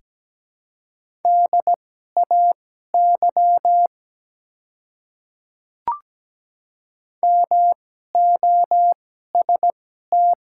Morse code, heard as a keyed tone, about 17 wpm, spaced as for 12 wpm, 690 Hz. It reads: DAY MOST
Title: 2x Character Spacing - Top 100 Words - 17wpm